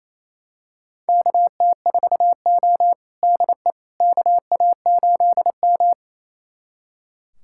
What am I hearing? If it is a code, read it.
Morse code, heard as a keyed tone, about 29 wpm, 690 Hz. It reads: KT4O DE KA8M
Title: USA Callsign DE Callsign - 4 Characters - 28wpm